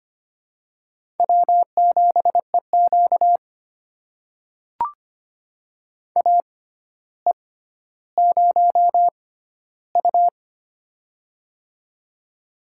Morse code, heard as a keyed tone, about 25 wpm, spaced as for 13 wpm, 700 Hz. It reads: W7EQ AE0U
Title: USA Call Signs - 4 Characters - 6x Character Spacing - 25wpm